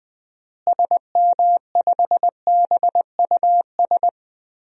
Morse code, heard as a keyed tone, about 20 wpm, 700 Hz. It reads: SM5BUS
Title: DXCC-100 Call Signs - 6 Characters - Repeated 2x - 20wpm